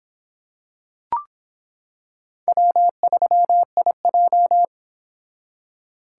Morse code, heard as a keyed tone, about 26 wpm, 700 Hz. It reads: W3IJ